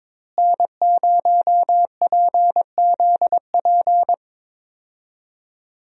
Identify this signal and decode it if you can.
Morse code, heard as a keyed tone, about 22 wpm, 700 Hz. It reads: N0PZP
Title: USA Call Signs - 5 Characters - Repeated 3x - 22wpm